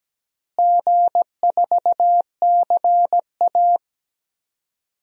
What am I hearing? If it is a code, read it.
Morse code, heard as a keyed tone, about 17 wpm, 700 Hz. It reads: G4CA
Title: DXCC-100 Call Signs - 4 Characters - Repeated 2x - 17wpm